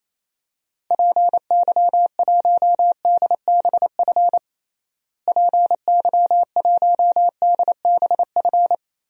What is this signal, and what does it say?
Morse code, heard as a keyed tone, about 28 wpm, 700 Hz. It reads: PY1DBF PY1DBF
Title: DXCC-100 Call Signs - 6 Characters - Repeated 2x - 28wpm